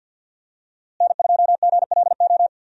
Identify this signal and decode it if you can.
Morse code, heard as a keyed tone, about 47 wpm, 700 Hz. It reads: N2GLO
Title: USA Call Signs - 5 Characters - Repeated 2x - 50wpm